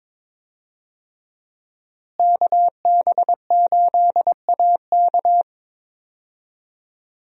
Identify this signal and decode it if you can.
Morse code, heard as a keyed tone, about 22 wpm, 700 Hz. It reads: KB8AK
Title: USA Call Signs - 5 Characters - Repeated 3x - 22wpm